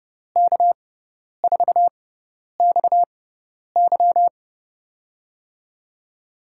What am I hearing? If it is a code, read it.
Morse code, heard as a keyed tone, about 30 wpm, spaced as for 10 wpm, 700 Hz. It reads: K4XY
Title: USA Call Signs - 5 Characters - 6x Character Spacing - 30wpm